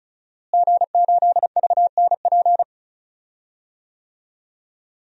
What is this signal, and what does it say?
Morse code, heard as a keyed tone, about 36 wpm, 700 Hz. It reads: G8VNP
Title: DXCC-100 Call Signs - 5 Characters - Repeated 3x - 35wpm